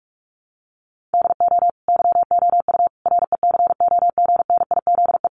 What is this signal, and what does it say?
Morse code, heard as a keyed tone, about 44 wpm, 700 Hz. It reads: DO YOU RECOGNIZE
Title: Sentences from Top 1400 Words 45wpm